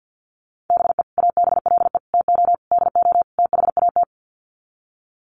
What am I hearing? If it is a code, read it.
Morse code, heard as a keyed tone, about 48 wpm, 700 Hz. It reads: BE ABLE TO DO THAT
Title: Sets of 5 Words Encore2 50wpm